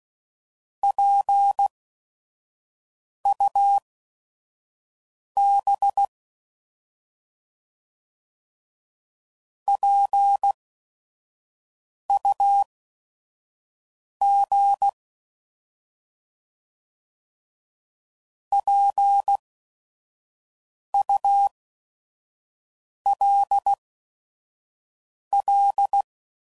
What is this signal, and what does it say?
Morse code, heard as a keyed tone, about 16 wpm, spaced as for 5 wpm, 800 Hz. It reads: PUB PUG PULL
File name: Prefixes / By First Letter / Ps